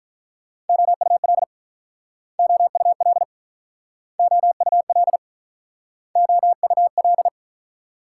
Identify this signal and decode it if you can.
Morse code, heard as a keyed tone, about 48 wpm, 700 Hz. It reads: OUL OUL OUL OUL